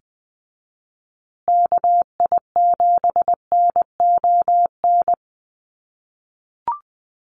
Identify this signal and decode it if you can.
Morse code, heard as a keyed tone, about 20 wpm, 700 Hz. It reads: KI7NON
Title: USA Call Signs Repeated 3x 20wpm